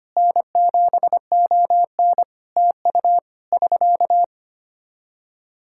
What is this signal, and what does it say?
Morse code, heard as a keyed tone, about 25 wpm, 700 Hz. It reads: N7ON TU <SK>
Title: Practice QSO - Exchange 4 - Ending - 25wpm